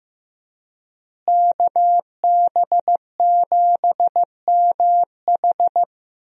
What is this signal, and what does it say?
Morse code, heard as a keyed tone, about 15 wpm, 700 Hz. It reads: KB7MH
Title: USA Call Signs - 5 Characters - Repeated 3x - 15wpm